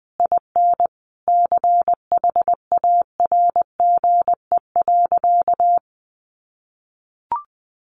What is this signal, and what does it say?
Morse code, heard as a keyed tone, about 20 wpm, 700 Hz. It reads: IN CHARGE.
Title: Sentences from Top 900 Words 20wpm